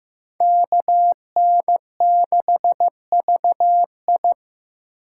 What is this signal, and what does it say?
Morse code, heard as a keyed tone, about 18 wpm, 700 Hz. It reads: KN6VI
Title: Speed Racing - USA 2x2 Call Signs 15wpm